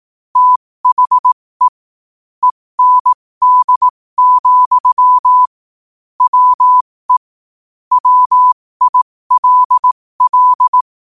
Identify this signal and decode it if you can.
Morse code, heard as a keyed tone, about 18 wpm, spaced as for 15 wpm, 1000 Hz. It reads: THE END, WE WILL
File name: qotdfiles / 15wpm